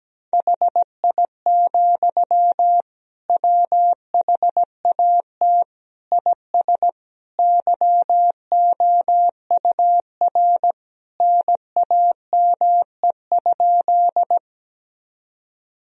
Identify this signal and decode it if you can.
Morse code, heard as a keyed tone, about 17 wpm, 700 Hz. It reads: HI, WHAT IS YOUR NAME?
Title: Sentences from Top 1000 Words - Review Part 3 - 17wpm